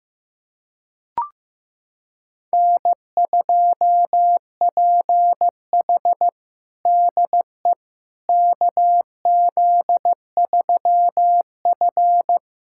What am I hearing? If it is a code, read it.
Morse code, heard as a keyed tone, about 15 wpm, 700 Hz. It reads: N2PH DE KZ3F
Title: USA Callsign DE Callsign - 4 Characters - 15wpm